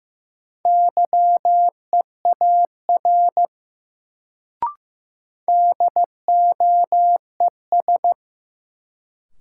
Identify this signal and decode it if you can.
Morse code, heard as a keyed tone, about 15 wpm, 700 Hz. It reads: YEAR DOES
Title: Sentences from Top 200 Words - New Word Drill - 15wpm